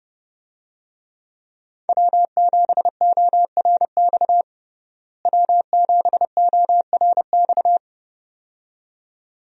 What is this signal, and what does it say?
Morse code, heard as a keyed tone, about 30 wpm, 700 Hz. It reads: W7ORX W7ORX
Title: USA Call Signs - 5 Characters - Repeated 2x - 30wpm